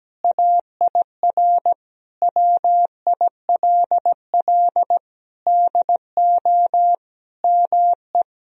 Morse code, heard as a keyed tone, about 17 wpm, 700 Hz. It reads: AIR WILL DO ME
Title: Sentences from Top 1000 Words - Review Part 1 - 17wpm